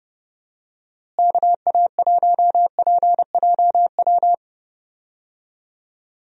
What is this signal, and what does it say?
Morse code, heard as a keyed tone, about 30 wpm, 700 Hz. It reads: KA1PJW
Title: USA Call Signs - 6 Characters - Repeated 3x - 30wpm